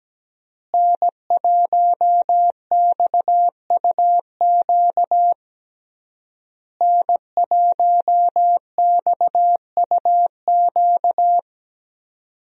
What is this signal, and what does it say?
Morse code, heard as a keyed tone, about 17 wpm, 700 Hz. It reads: N1XUQ N1XUQ
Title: USA Call Signs - 5 Characters - Repeated 3x - 17wpm